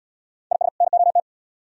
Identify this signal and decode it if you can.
Morse code, heard as a keyed tone, about 74 wpm, 700 Hz. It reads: IS NOT